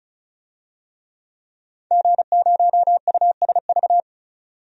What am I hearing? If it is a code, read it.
Morse code, heard as a keyed tone, about 36 wpm, 700 Hz. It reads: G0USV
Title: DXCC-100 Call Signs - 5 Characters - 35wpm